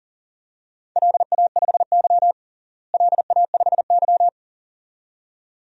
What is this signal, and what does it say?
Morse code, heard as a keyed tone, about 40 wpm, 700 Hz. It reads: LA5Y LA5Y
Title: DXCC-100 Call Signs - 4 Characters - Repeated 2x - 40wpm